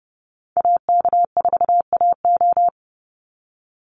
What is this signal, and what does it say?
Morse code, heard as a keyed tone, about 30 wpm, 700 Hz. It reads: AK4AO
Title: Call Signs Repeated 3x - 30wpm